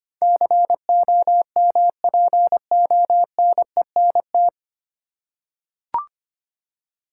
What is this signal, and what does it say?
Morse code, heard as a keyed tone, about 25 wpm, 700 Hz. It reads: COMPONENT
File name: Nine Letter Words - 25wpm